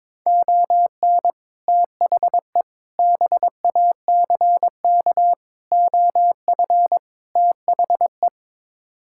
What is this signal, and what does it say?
Morse code, heard as a keyed tone, about 22 wpm, 700 Hz. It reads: ON THE BACK OF THE
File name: practice / 22wpm / Build A Phrase - 5-Word Phrases - 22wpm